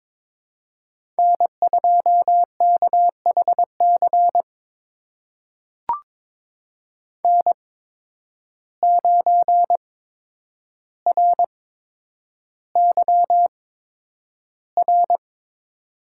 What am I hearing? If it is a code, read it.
Morse code, heard as a keyed tone, about 22 wpm, spaced as for 9 wpm, 700 Hz. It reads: N2KHC N9RYR
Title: USA Call Signs - 5 Characters - 8x Character Spacing - 22wpm